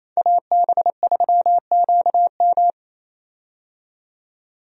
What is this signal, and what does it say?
Morse code, heard as a keyed tone, about 28 wpm, 700 Hz. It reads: AB3QM
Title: USA Call Signs - 5 Characters - 28wpm